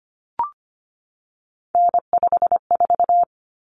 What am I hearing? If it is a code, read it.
Morse code, heard as a keyed tone, about 25 wpm, 700 Hz. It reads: N54
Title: Common Cut Numbers - 3-Digit 25wpm